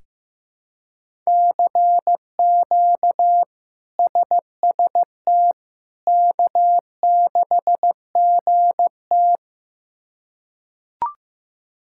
Morse code, heard as a keyed tone, about 15 wpm, 700 Hz. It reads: CQ SST K6GT
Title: K1USN SST QSO Practice - 15wpm